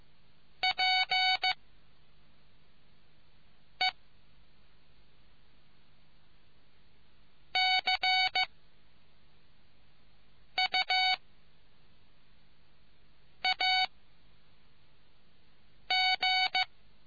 Morse code, heard as a keyed tone, about 15 wpm, 750 Hz. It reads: P E C U A G